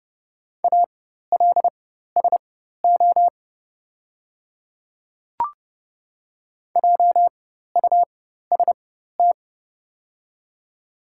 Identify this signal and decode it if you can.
Morse code, heard as a keyed tone, about 30 wpm, spaced as for 14 wpm, 700 Hz. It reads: ALSO JUST
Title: 4x Character Spacing - Top 200 Words - 30wpm